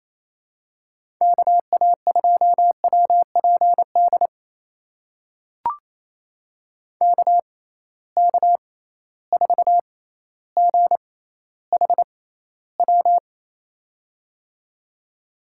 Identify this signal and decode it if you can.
Morse code, heard as a keyed tone, about 28 wpm, spaced as for 14 wpm, 700 Hz. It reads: KA2WPD KK4GHW
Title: USA Call Signs - 6x Character Spacing - 28wpm